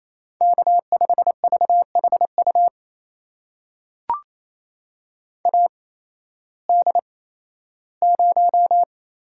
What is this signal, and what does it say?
Morse code, heard as a keyed tone, about 28 wpm, spaced as for 15 wpm, 700 Hz. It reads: K5VHU AD0
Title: USA Call Signs - 5 Characters - 8x Character Spacing - 28wpm